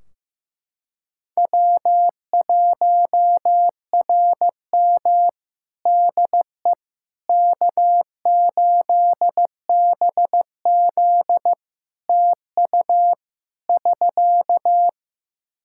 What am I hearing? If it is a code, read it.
Morse code, heard as a keyed tone, about 15 wpm, 700 Hz. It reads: W1RM DE K8BZ TU <SK>
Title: Practice QSO - Exchange 4 - Ending - 15wpm